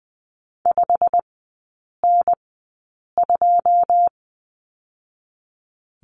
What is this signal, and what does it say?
Morse code, heard as a keyed tone, about 20 wpm, spaced as for 8 wpm, 700 Hz. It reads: 5N2
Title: 166 Lesson 27e - Number 7 - Three Characters 20wpm